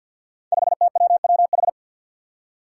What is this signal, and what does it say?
Morse code, heard as a keyed tone, about 48 wpm, 700 Hz. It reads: 5TWWH
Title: Five Challenging Letter-Number Combinations 50wpm